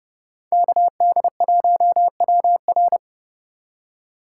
Morse code, heard as a keyed tone, about 38 wpm, 700 Hz. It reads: KD1WR